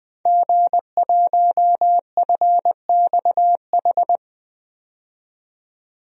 Morse code, heard as a keyed tone, about 20 wpm, 700 Hz. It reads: G1FXH
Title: DXCC-100 Call Signs - 5 Characters - 20wpm